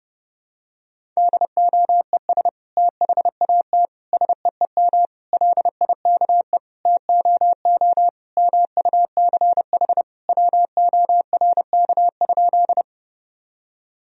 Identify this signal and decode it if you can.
Morse code, heard as a keyed tone, about 30 wpm, 700 Hz. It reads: DOES THAT SEEM LIKE TOO MUCH WORK?
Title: Sentences from Top 1000 Words - Review Part 1 - 30wpm